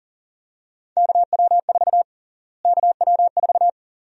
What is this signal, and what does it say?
Morse code, heard as a keyed tone, about 40 wpm, 700 Hz. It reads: KW4 KW4
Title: USA Call Sign Prefixes - Repeated 2x - 40wpm